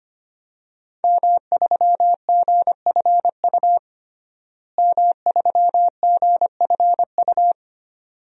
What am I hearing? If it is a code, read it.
Morse code, heard as a keyed tone, about 25 wpm, 700 Hz. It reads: M3GFU M3GFU
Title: DXCC-100 Call Signs - 5 Characters - Repeated 2x - 25wpm